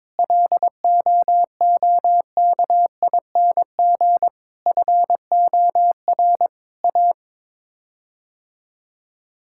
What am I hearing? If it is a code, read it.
Morse code, heard as a keyed tone, about 22 wpm, 700 Hz. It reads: LOOKING FOR A
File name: practice / 22wpm / Sets of 6 Words - 22wpm